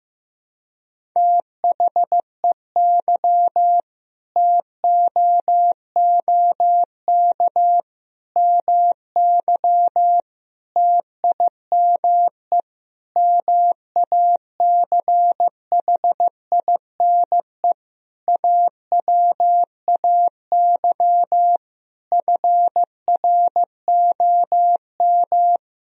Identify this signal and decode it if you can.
Morse code, heard as a keyed tone, about 15 wpm, 700 Hz. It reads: THEY TOOK MY TIME MACHINE AWAY FROM
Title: Sentences from Top 1000 Words - Review Part 2 - 15wpm